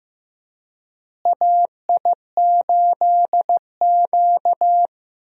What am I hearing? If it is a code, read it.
Morse code, heard as a keyed tone, about 15 wpm, 700 Hz. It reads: AI8Q
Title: USA Call Signs - 4 Characters - Repeated 3x - 15wpm